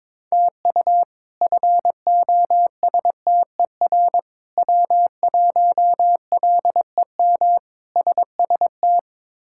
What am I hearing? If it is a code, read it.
Morse code, heard as a keyed tone, about 22 wpm, 700 Hz. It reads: TU FOSTER W1LEM SST
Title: K1USN SST QSO Practice - 22wpm